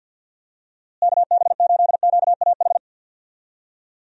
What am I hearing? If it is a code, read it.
Morse code, heard as a keyed tone, about 46 wpm, 700 Hz. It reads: KB8QAH
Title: USA Call Signs - 6 Characters - Repeated 2x - 50wpm